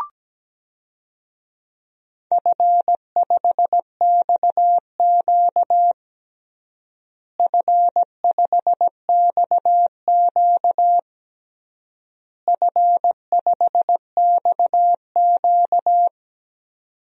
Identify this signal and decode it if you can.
Morse code, heard as a keyed tone, about 17 wpm, 700 Hz. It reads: F5XQ F5XQ F5XQ
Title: DXCC-100 Call Signs - 4 Characters - Repeated 3x - 17wpm